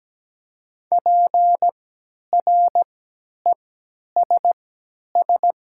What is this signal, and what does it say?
Morse code, heard as a keyed tone, about 17 wpm, spaced as for 10 wpm, 700 Hz. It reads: PRESS
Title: 3x Character Spacing - Top 500 Words - 17wpm